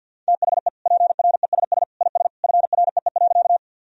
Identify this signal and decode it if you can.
Morse code, heard as a keyed tone, about 51 wpm, 700 Hz. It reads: THE PRESS IS FREE.
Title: Sentences from Top 1000 Words - Review Part 1 - 50wpm